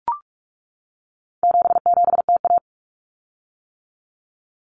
Morse code, heard as a keyed tone, about 44 wpm, 700 Hz. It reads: 77TA
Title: Common Cut Numbers - 4-Digit 45wpm